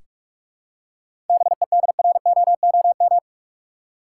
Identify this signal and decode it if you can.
Morse code, heard as a keyed tone, about 46 wpm, 700 Hz. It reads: BEDROOM
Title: Seven Letter Words - 45wpm